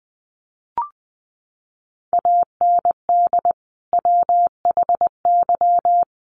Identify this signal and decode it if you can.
Morse code, heard as a keyed tone, about 20 wpm, 700 Hz. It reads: AND WHY
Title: Sentences from Top 300 Words 20wpm